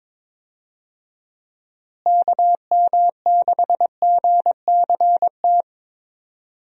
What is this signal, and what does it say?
Morse code, heard as a keyed tone, about 22 wpm, 700 Hz. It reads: KM6GCT
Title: USA Call Signs - 6 Characters - Repeated 2x - 22wpm